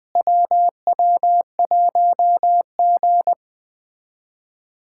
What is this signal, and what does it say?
Morse code, heard as a keyed tone, about 20 wpm, 700 Hz. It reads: WW1G
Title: USA Callsign DE Callsign - 4 Characters - 20wpm